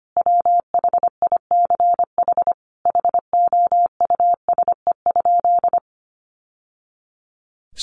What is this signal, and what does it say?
Morse code, heard as a keyed tone, about 25 wpm, 700 Hz. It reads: WHICH HOUSE?
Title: Sentences from Top 1100 Words 25wpm